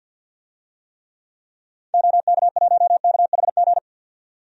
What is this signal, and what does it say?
Morse code, heard as a keyed tone, about 50 wpm, 700 Hz. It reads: OK1KHG
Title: DXCC-100 Call Signs - 6 Characters - 50wpm